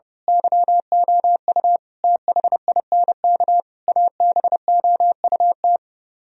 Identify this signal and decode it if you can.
Morse code, heard as a keyed tone, about 29 wpm, 700 Hz. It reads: YOU THINK ABOUT
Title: Speed-Racing - Sets of 5 Words 30wpm